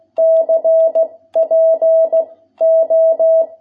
Morse code, heard as a keyed tone, about 16 wpm, 650 Hz. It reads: C P O
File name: jh8cbh / 2021cw